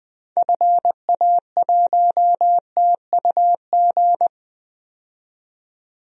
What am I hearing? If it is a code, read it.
Morse code, heard as a keyed tone, about 20 wpm, 700 Hz. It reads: FA1TUG
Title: DXCC-100 Call Signs - 6 Characters - 20wpm